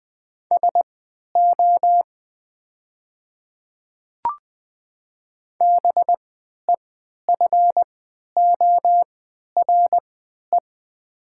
Morse code, heard as a keyed tone, about 20 wpm, spaced as for 11 wpm, 700 Hz. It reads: SO BEFORE
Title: 3x Character Spacing - Top 200 Words - 20wpm